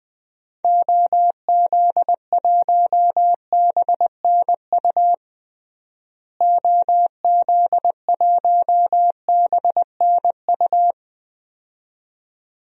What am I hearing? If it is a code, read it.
Morse code, heard as a keyed tone, about 20 wpm, 700 Hz. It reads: OZ1BNU OZ1BNU
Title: DXCC-100 Call Signs - 6 Characters - Repeated 2x - 20wpm